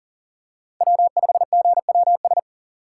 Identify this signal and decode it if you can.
Morse code, heard as a keyed tone, about 40 wpm, 700 Hz. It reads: W5GWS
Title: USA Call Signs - 5 Characters - 6x Character Spacing - 40wpm